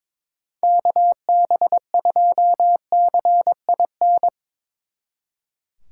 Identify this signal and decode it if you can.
Morse code, heard as a keyed tone, about 25 wpm, 700 Hz. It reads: KB2CIN
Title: Speed Racing - USA 2x3 Call Signs 22wpm